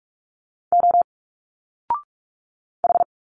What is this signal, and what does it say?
Morse code, heard as a keyed tone, about 45 wpm, 700 Hz. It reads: O H
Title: Single Letters 45wpm